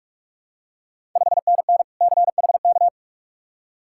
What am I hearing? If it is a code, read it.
Morse code, heard as a keyed tone, about 46 wpm, 700 Hz. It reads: 5NN CHK
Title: Ontario QSO Party Exchange 45wpm